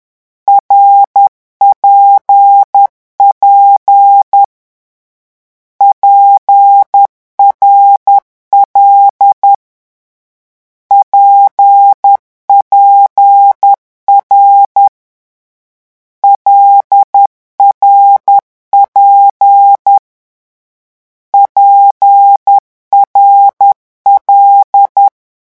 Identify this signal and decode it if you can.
Morse code, heard as a keyed tone, about 10 wpm, 800 Hz. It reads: RPP PRL PPR LRP PRL
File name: morse mp3 / lecon6-vitesse 10